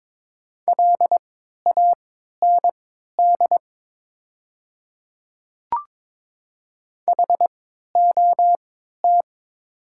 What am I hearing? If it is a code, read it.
Morse code, heard as a keyed tone, about 22 wpm, spaced as for 13 wpm, 700 Hz. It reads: LAND HOT
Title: 3x Character Spacing - Top 200 Words - 22wpm